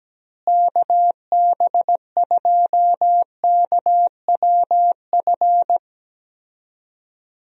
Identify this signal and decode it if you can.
Morse code, heard as a keyed tone, about 17 wpm, 700 Hz. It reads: KB2KWF
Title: USA Call Signs - 6 Characters - Repeated 2x - 17wpm